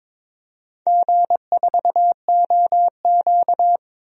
Morse code, heard as a keyed tone, about 22 wpm, 700 Hz. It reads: G4OQ
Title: DXCC-100 Call Signs - 4 Characters - 22wpm